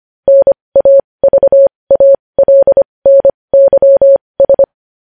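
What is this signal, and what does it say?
Morse code, heard as a keyed tone, about 25 wpm, spaced as for 20 wpm, 550 Hz. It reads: NAVALNYS
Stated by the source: Morse practice of news headlines at 20 wpm.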